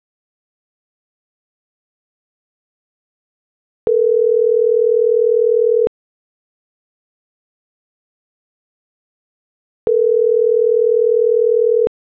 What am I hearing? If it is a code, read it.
Ringback tone (North America), 440 + 480 Hz.
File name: CENTRAL NMP / out isabelponciano